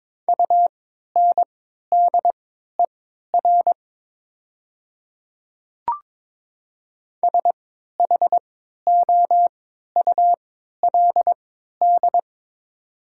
Morse code, heard as a keyed tone, about 22 wpm, spaced as for 12 wpm, 700 Hz. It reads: UNDER SHOULD
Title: 3x Character Spacing - Top 300 Words - 22wpm